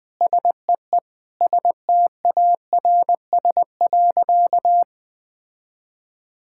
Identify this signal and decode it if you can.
Morse code, heard as a keyed tone, about 20 wpm, 700 Hz. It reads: SEE STARS.
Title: Sentences with 4-Word Phrases - Spoken Beginning - 20wpm